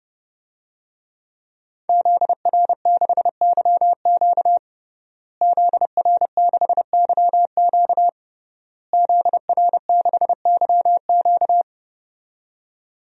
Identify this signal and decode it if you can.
Morse code, heard as a keyed tone, about 30 wpm, 700 Hz. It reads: ZR6YQ ZR6YQ ZR6YQ
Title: DXCC-100 Call Signs - 5 Characters - Repeated 3x - 30wpm